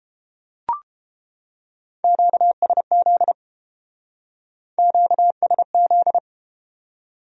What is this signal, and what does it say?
Morse code, heard as a keyed tone, about 31 wpm, 700 Hz. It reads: QSZ QSZ